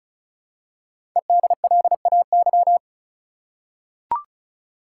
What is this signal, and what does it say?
Morse code, heard as a keyed tone, about 36 wpm, 700 Hz. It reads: EDLAY
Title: Code Groups - 5 Letters - 35wpm